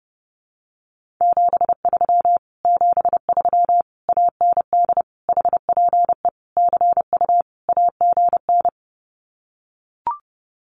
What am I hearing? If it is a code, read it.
Morse code, heard as a keyed tone, about 30 wpm, 700 Hz. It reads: 73 73 AND HPE CU AGN
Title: Common QSO Phrases Encore 30wpm